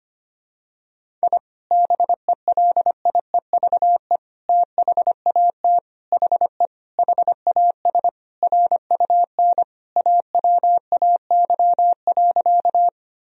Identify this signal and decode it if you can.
Morse code, heard as a keyed tone, about 25 wpm, 700 Hz. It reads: I BELIEVE THAT HE HAS RUN AWAY.
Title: Sentences from Top 1000 Words - Review Part 1 - 25wpm